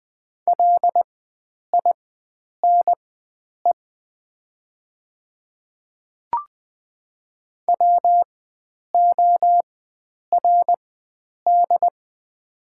Morse code, heard as a keyed tone, about 20 wpm, spaced as for 9 wpm, 700 Hz. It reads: LINE WORD
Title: 4x Character Spacing - Top 300 Words - 20wpm